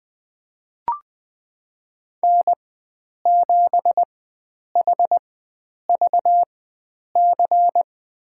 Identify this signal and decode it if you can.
Morse code, heard as a keyed tone, about 20 wpm, spaced as for 9 wpm, 700 Hz. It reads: N7HVC
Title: USA Call Signs - 4x Character Spacing - 20wpm